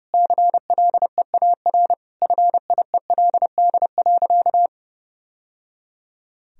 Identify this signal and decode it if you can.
Morse code, heard as a keyed tone, about 30 wpm, 700 Hz. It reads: CLEAR FIELD.
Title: Sentences from Top 1000 Words - Review Part 2 - 30wpm